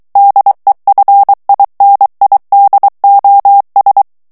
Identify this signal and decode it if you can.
Morse code, heard as a keyed tone, about 23 wpm, 800 Hz. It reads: DEFINIDOS
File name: morse mp3 / ESPAGNOL TEXTE EN CLAIR 2 VITESSE 22 MOTS MINUTES